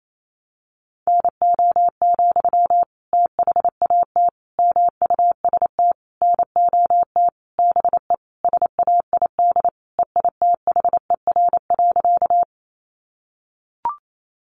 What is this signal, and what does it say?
Morse code, heard as a keyed tone, about 28 wpm, 700 Hz. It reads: NO, THAT MUST NOT BE SAID EITHER.
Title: Sentences from Top 600 Words 28wpm